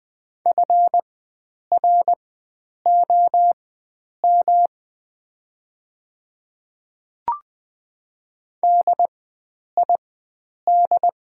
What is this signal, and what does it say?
Morse code, heard as a keyed tone, about 21 wpm, spaced as for 9 wpm, 700 Hz. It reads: FROM DID
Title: 4x Character Spacing - Top 100 Words - 20wpm